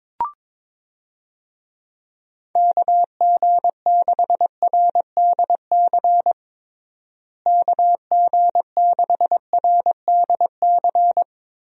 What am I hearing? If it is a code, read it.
Morse code, heard as a keyed tone, about 22 wpm, 700 Hz. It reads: KG6RDC KG6RDC
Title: USA Call Signs - 6 Characters - Repeated 2x - 22wpm